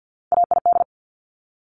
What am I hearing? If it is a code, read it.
Morse code, heard as a keyed tone, about 50 wpm, 700 Hz. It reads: AID
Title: Top Three Letter Combinations in Words 50wpm